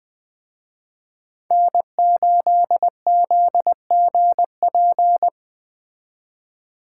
Morse code, heard as a keyed tone, about 20 wpm, 700 Hz. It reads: N8ZGP